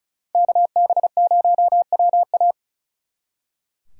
Morse code, heard as a keyed tone, about 45 wpm, 700 Hz. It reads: KB0WA